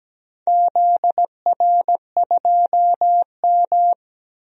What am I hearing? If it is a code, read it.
Morse code, heard as a keyed tone, about 17 wpm, 700 Hz. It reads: ZR2M
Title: DXCC-100 Call Signs - 4 Characters - 17wpm